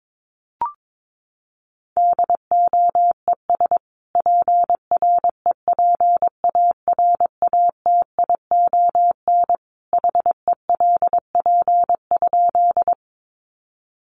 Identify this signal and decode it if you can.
Morse code, heard as a keyed tone, about 22 wpm, 700 Hz. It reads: DOES PREPARATION HELP?
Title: Sentences from Top 1400 Words 22wpm